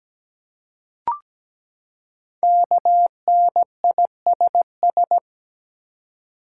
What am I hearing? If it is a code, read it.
Morse code, heard as a keyed tone, about 17 wpm, 700 Hz. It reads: KNISS
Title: Code Groups - 5 Letters - 17wpm